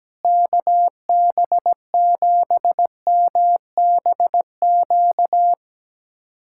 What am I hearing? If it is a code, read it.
Morse code, heard as a keyed tone, about 17 wpm, 700 Hz. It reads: KB7MBQ
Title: USA Callsign DE Callsign - 6 Characters - 17wpm